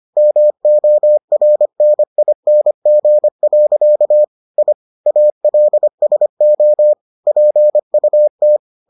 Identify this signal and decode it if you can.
Morse code, heard as a keyed tone, about 25 wpm, 600 Hz. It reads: MORNING. I ALSO PUT